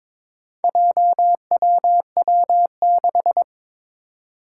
Morse code, heard as a keyed tone, about 22 wpm, 700 Hz. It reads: JWW6
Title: Four Challenging Letter-Number Combinations 22wpm